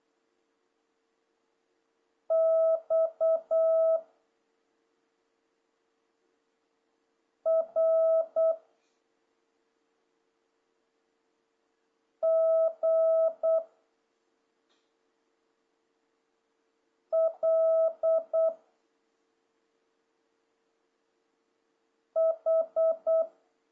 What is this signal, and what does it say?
Morse code, heard as a keyed tone, about 8 wpm, 650 Hz. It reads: X R G L H
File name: jh8cbh / 2026cw